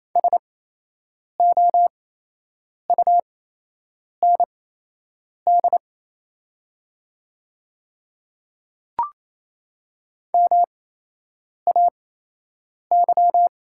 Morse code, heard as a keyed tone, about 28 wpm, spaced as for 8 wpm, 700 Hz. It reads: SOUND MAY
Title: 8x Character Spacing - Top 200 Words - 28wpm